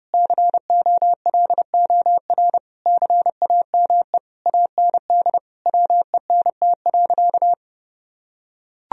Morse code, heard as a keyed tone, about 30 wpm, 700 Hz. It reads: COLOR CAME AND WENT.
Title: Sentences from Top 1000 Words - Review Part 2 - 30wpm